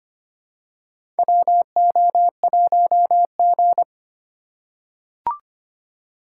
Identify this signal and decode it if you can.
Morse code, heard as a keyed tone, about 25 wpm, 700 Hz. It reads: WO1G
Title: USA Call Signs - 4 Characters - 4x Character Spacing - 25wpm